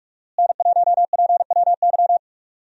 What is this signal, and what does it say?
Morse code, heard as a keyed tone, about 46 wpm, 700 Hz. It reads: N1PWY
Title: USA Call Signs - 5 Characters - Repeated 2x - 45wpm